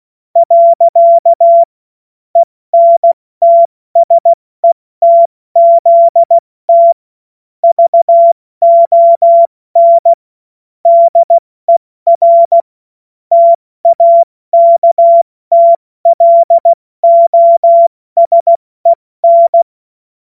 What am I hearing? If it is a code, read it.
Morse code, heard as a keyed tone, about 16 wpm, 680 Hz. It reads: . ENTSETZT VON DER TAKTLOSEN
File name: Morsekurs / Witze / Witz 05 Dezember 2023 70 F80